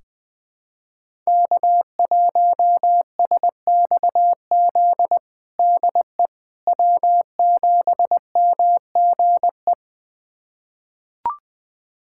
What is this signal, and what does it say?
Morse code, heard as a keyed tone, about 20 wpm, 700 Hz. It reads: K1SXZ DE W7MGE
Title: USA Callsign DE Callsign - 5 Characters - 20wpm